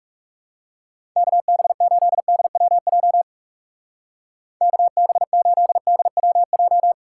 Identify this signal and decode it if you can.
Morse code, heard as a keyed tone, about 36 wpm, 700 Hz. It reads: KB8DWJ KB8DWJ
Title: Speed Racing - USA 2x3 Call Signs 30wpm